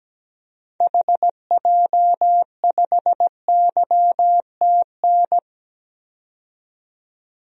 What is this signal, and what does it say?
Morse code, heard as a keyed tone, about 17 wpm, 700 Hz. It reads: HJ5YTN
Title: DXCC-100 Call Signs - 6 Characters - 17wpm